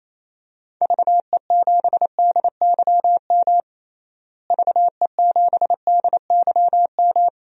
Morse code, heard as a keyed tone, about 28 wpm, 700 Hz. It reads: VE7DYM VE7DYM
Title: DXCC-100 Call Signs - 6 Characters - Repeated 2x - 28wpm